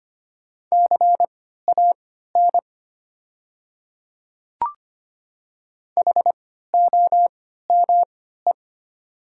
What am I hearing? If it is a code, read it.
Morse code, heard as a keyed tone, about 25 wpm, spaced as for 14 wpm, 700 Hz. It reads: CAN HOME
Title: 3x Character Spacing - Top 200 Words - 25wpm